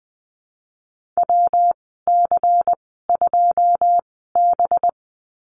Morse code, heard as a keyed tone, about 20 wpm, spaced as for 14 wpm, 700 Hz. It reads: WC2B
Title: 193 Lesson 31g - Letter v - Call Signs 20wpm